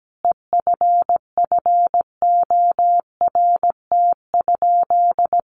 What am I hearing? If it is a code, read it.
Morse code, heard as a keyed tone, about 17 wpm, 700 Hz. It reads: EFFORT?
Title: Sentences from Top 1200 Words 17wpm